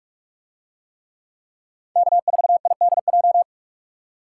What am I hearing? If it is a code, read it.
Morse code, heard as a keyed tone, about 43 wpm, 700 Hz. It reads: K4IDJ